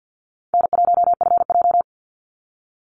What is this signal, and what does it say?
Morse code, heard as a keyed tone, about 49 wpm, 700 Hz. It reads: N1FJ
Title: NA SOTA Chaser Callsigns 50wpm